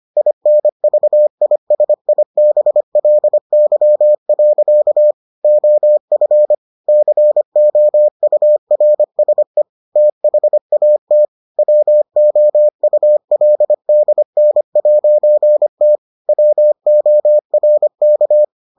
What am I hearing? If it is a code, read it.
Morse code, heard as a keyed tone, about 25 wpm, 600 Hz. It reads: INVISIBLY. OF COURSE THAT WOULDN'T WORK